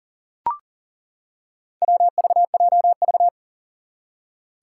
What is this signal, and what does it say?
Morse code, heard as a keyed tone, about 40 wpm, 700 Hz. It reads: WVJV
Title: Four Challenging Letter-Number Combinations 40wpm